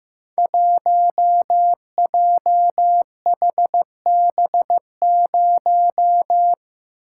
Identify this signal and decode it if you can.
Morse code, heard as a keyed tone, about 15 wpm, 700 Hz. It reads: 1JHB0
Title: Code Groups - 5 Letter:Numbers - 15wpm